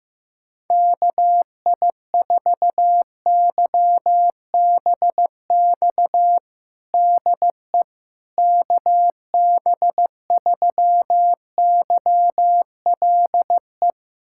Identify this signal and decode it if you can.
Morse code, heard as a keyed tone, about 15 wpm, 700 Hz. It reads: KI4YBX DE KB3YLE
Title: USA Callsign DE Callsign - 6 Characters - 15wpm